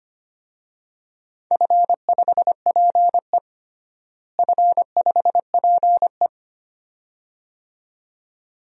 Morse code, heard as a keyed tone, about 25 wpm, 700 Hz. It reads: F5PE F5PE
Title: DXCC-100 Call Signs - 4 Characters - Repeated 2x - 25wpm